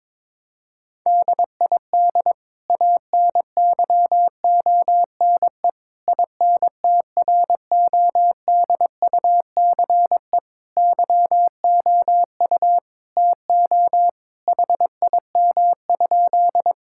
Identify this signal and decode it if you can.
Morse code, heard as a keyed tone, about 22 wpm, 700 Hz. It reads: DID ANYONE INTRODUCE YOU TO HIM?
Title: Sentences from Top 1600 Words 22wpm